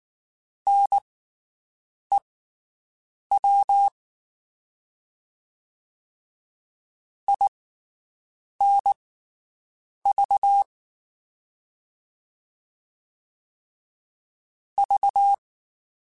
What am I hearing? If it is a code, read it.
Morse code, heard as a keyed tone, about 19 wpm, spaced as for 7 wpm, 800 Hz. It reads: NEW INV V
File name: QSOs Spaced